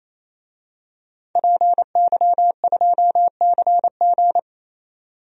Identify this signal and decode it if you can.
Morse code, heard as a keyed tone, about 28 wpm, 700 Hz. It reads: PY2CG
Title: DXCC-100 Call Signs - 5 Characters - 28wpm